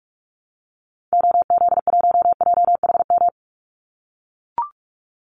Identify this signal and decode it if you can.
Morse code, heard as a keyed tone, about 44 wpm, 700 Hz. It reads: OZ1JHM
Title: Call Signs - 45wpm